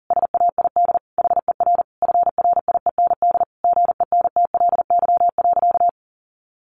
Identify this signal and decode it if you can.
Morse code, heard as a keyed tone, about 40 wpm, 700 Hz. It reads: SAID HER FRIEND GENTLY.
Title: Sentences from Top 1500 Words 40wpm